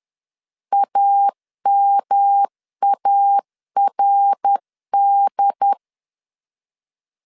Morse code, heard as a keyed tone, about 10 wpm, 800 Hz. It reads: AMARD
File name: 10wpm